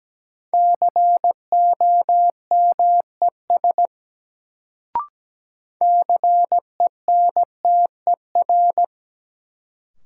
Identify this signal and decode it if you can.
Morse code, heard as a keyed tone, about 17 wpm, 700 Hz. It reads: COMES CENTER
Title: Sets of 3 Words - New Word Drill - 17wpm